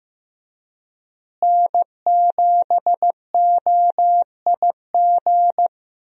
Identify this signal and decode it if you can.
Morse code, heard as a keyed tone, about 15 wpm, 700 Hz. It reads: N7OIG
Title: USA Call Signs - 5 Characters - Repeated 3x - 15wpm